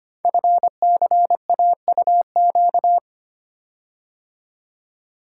Morse code, heard as a keyed tone, about 25 wpm, 700 Hz. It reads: FCAUQ
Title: Code Groups - 5 Letters - 25wpm